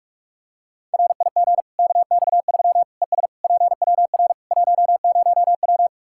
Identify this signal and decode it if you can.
Morse code, heard as a keyed tone, about 43 wpm, 700 Hz. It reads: RIG KX3 ES PWR 10W
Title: Practice QSO - Exchange 2 - Info - 45wpm